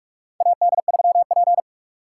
Morse code, heard as a keyed tone, about 47 wpm, 700 Hz. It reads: AD3P
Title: USA Call Signs - 2x Character Spacing - 45wpm